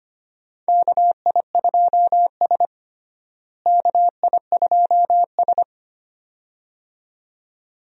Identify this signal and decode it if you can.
Morse code, heard as a keyed tone, about 25 wpm, 700 Hz. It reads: KI2S KI2S
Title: USA Call Signs - 4 Characters - Repeated 2x - 25wpm